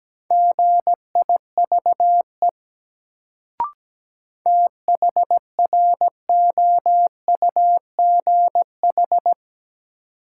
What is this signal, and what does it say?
Morse code, heard as a keyed tone, about 17 wpm, 700 Hz. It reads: GIVE THROUGH
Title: Sentences from Top 200 Words - New Word Drill - 17wpm